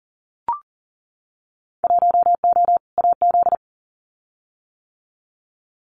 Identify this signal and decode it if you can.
Morse code, heard as a keyed tone, about 39 wpm, 700 Hz. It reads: 1O AZ
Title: Winter Field Day Exchanges 40wpm